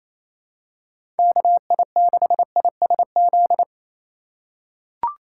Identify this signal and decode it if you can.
Morse code, heard as a keyed tone, about 28 wpm, 700 Hz. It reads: KI6ISZ
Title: USA Call Signs - 6 Characters - 3x Character Spacing - 28wpm